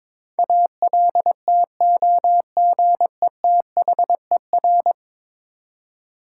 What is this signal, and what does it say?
Morse code, heard as a keyed tone, about 22 wpm, 700 Hz. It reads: ALTOGETHER
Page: morse-code-ninja-practice-22wpm